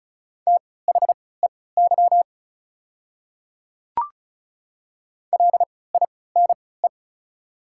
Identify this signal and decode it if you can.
Morse code, heard as a keyed tone, about 36 wpm, spaced as for 20 wpm, 700 Hz. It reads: THEY LINE
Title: 3x Character Spacing - Top 200 Words - 35wpm